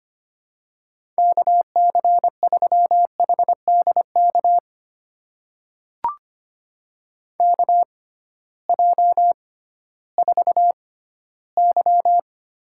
Morse code, heard as a keyed tone, about 25 wpm, spaced as for 15 wpm, 700 Hz. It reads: KC3HDK KJ4Y
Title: USA Call Signs - 6 Characters - 6x Character Spacing - 25wpm